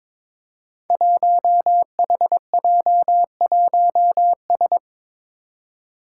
Morse code, heard as a keyed tone, about 23 wpm, 700 Hz. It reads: 1HJ1S
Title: Five Challenging Letter-Number Combinations 22wpm